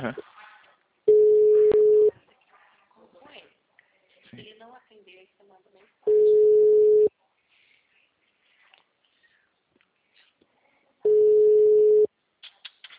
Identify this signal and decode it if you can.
Ringback tone (Europe, 425 Hz), with speech.